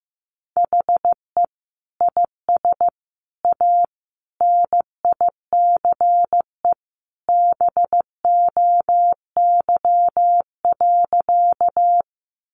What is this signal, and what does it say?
Morse code, heard as a keyed tone, about 14 wpm, 700 Hz. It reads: HE IS A NICE BOY.
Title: Sentences from Top 1100 Words 15wpm